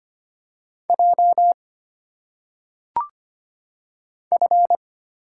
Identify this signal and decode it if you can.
Morse code, heard as a keyed tone, about 25 wpm, 700 Hz. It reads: J F
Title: Single Letters - Phonetic Pronunciation 25wpm